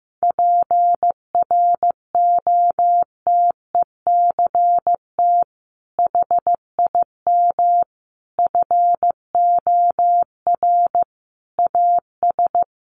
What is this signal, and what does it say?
Morse code, heard as a keyed tone, about 14 wpm, 700 Hz. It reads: PROTECT HIM FOR AS
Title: Sentences from Top 900 Words 15wpm